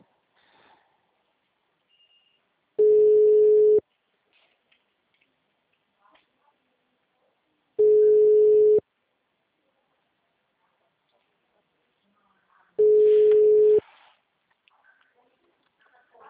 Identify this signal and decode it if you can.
Ringback tone (Europe, 425 Hz).